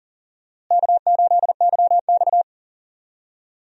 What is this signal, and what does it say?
Morse code, heard as a keyed tone, about 40 wpm, 700 Hz. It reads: K8YX
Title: USA Call Signs - 4 Characters - Repeated 2x - 40wpm